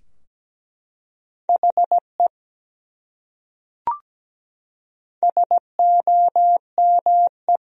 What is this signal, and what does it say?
Morse code, heard as a keyed tone, about 17 wpm, 700 Hz. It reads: HE SOME
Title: Top 100 Words Repeated 17wpm